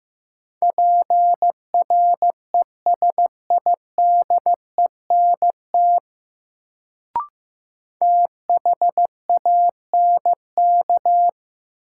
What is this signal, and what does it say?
Morse code, heard as a keyed tone, about 15 wpm, 700 Hz. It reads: PRESIDENT THANK
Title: Sets of 2 Words - New Word Drill - 15wpm